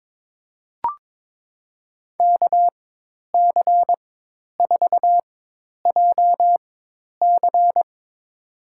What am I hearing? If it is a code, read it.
Morse code, heard as a keyed tone, about 22 wpm, spaced as for 10 wpm, 700 Hz. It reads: KC4JC
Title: USA Call Signs - 4x Character Spacing - 22wpm